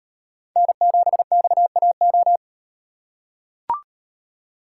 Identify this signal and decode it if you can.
Morse code, heard as a keyed tone, about 38 wpm, 700 Hz. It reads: N7XAO